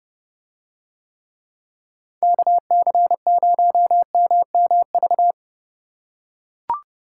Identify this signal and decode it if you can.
Morse code, heard as a keyed tone, about 27 wpm, 700 Hz. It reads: KC0MMV